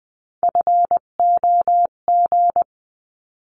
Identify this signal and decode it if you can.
Morse code, heard as a keyed tone, about 20 wpm, 700 Hz. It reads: FOG
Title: 258 Lesson 42c - Words 16-20wpm